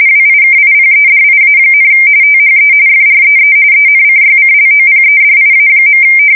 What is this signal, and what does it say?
RTTY, about 45.45 baud. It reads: WIKIPEDIA, THE FREE ENCYCLOPEDIA.
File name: fsk170 / USS-Missouri